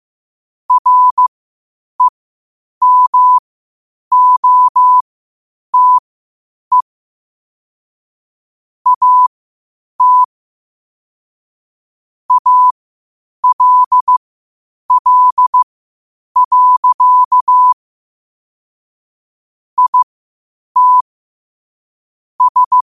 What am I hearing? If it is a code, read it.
Morse code, heard as a keyed tone, about 15 wpm, spaced as for 8 wpm, 1000 Hz. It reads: REMOTE AT ALL. IT S
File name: qotdfiles / 8wpm